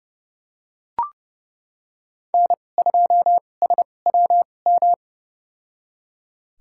Morse code, heard as a keyed tone, about 30 wpm, spaced as for 22 wpm, 700 Hz. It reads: N2SWM
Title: USA Call Signs - 5 Characters - 2x Character Spacing - 30wpm